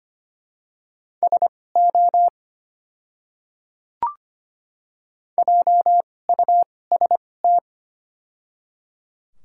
Morse code, heard as a keyed tone, about 25 wpm, spaced as for 18 wpm, 690 Hz. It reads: SO JUST
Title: 2x Character Spacing - Top 500 Words - 25wpm